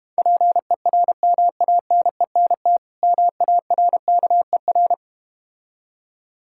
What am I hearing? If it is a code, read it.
Morse code, heard as a keyed tone, about 32 wpm, 690 Hz. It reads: PERMANENT MARKER